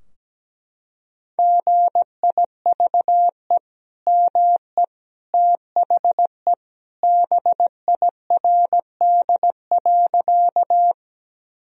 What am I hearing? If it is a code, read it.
Morse code, heard as a keyed tone, about 17 wpm, 700 Hz. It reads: GIVE ME THE BIRD.
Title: Sentences from Top 1000 Words - Review Part 2 - 17wpm